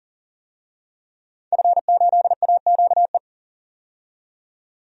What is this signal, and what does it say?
Morse code, heard as a keyed tone, about 40 wpm, 700 Hz. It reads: F8AQE
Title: DXCC-100 Call Signs - 5 Characters - 40wpm